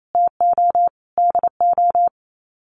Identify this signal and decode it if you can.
Morse code, heard as a keyed tone, about 28 wpm, 700 Hz. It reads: TO DO
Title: Sets of 5 Words 28wpm